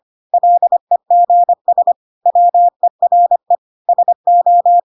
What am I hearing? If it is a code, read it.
Morse code, heard as a keyed tone, about 25 wpm, 700 Hz. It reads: LEGS WERE SO
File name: Pinocchio / Pinocchio ch3 25 25 ik1hge